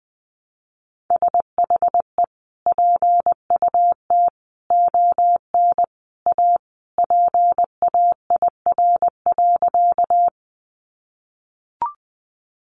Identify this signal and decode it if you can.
Morse code, heard as a keyed tone, about 20 wpm, 700 Hz. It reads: SHE PUT ON A PAIR.
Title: Sentences from Top 1200 Words 20wpm